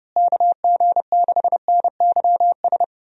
Morse code, heard as a keyed tone, about 30 wpm, 700 Hz. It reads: KG6NYS
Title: USA Call Signs - 6 Characters - Repeated 3x - 30wpm